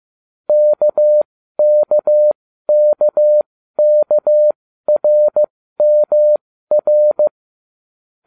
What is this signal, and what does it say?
Morse code, heard as a keyed tone, about 15 wpm, spaced as for 12 wpm, 600 Hz. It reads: KKKKRMR